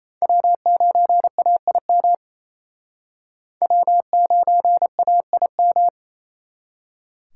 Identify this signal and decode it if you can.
Morse code, heard as a keyed tone, about 34 wpm, 700 Hz. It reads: W9AIM W9AIM
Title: Speed Racing - USA 1x3 Call Signs 28wpm